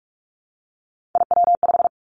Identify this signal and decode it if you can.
Morse code, heard as a keyed tone, about 46 wpm, 700 Hz. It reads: IW5
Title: Three Challenging Letter-Number Combinations 45wpm